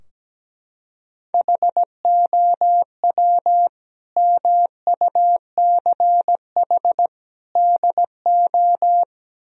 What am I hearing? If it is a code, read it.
Morse code, heard as a keyed tone, about 17 wpm, 700 Hz. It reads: HOW MUCH DO
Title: Sentences from Top 1000 Words - Review Part 3 - 17wpm